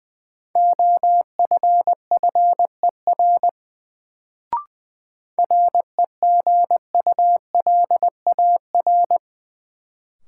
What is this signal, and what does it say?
Morse code, heard as a keyed tone, about 20 wpm, 700 Hz. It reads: OFFER REGULAR
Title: Sentences from Top 900 Words - New Word Drill - 20wpm